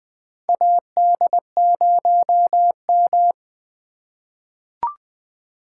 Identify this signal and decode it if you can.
Morse code, heard as a keyed tone, about 20 wpm, 700 Hz. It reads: AD0M
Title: USA Call Signs - 3x Character Spacing - 20wpm